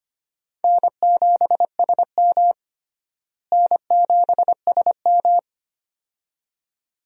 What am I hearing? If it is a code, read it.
Morse code, heard as a keyed tone, about 25 wpm, 700 Hz. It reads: N7SM N7SM
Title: USA Call Signs - 4 Characters - Repeated 2x - 25wpm